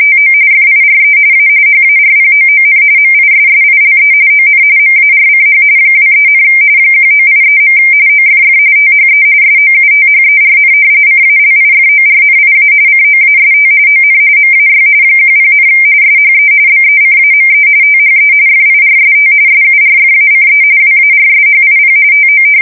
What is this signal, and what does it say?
RTTY, about 45.45 baud. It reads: TO FIRE A HARPOON MISSILE ON MARCH 23, 1983. OFF THE SOUTHERN CALIFORNIA COAST ON MAY 10, SHE BECAME THE FIRST SURFACE WARSHIP TO
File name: fsk170 / battleshipnj